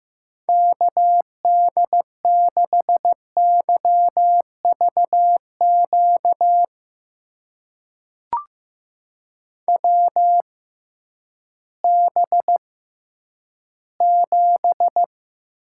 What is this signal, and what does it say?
Morse code, heard as a keyed tone, about 14 wpm, spaced as for 10 wpm, 700 Hz. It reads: KD6YVQ WB7
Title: USA Call Signs - 6 Characters - 6x Character Spacing - 15wpm